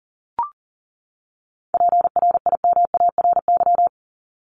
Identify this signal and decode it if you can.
Morse code, heard as a keyed tone, about 40 wpm, 700 Hz. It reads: PRIMARY
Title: LY Suffix - Top 100 Words with Base Words 40wpm